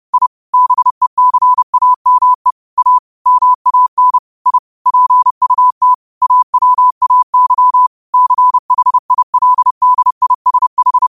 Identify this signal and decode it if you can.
Morse code, heard as a keyed tone, about 30 wpm, 1000 Hz. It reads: I BECAME A MAN I PUT AWAY CHILDISH
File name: qotdfiles / 30wpm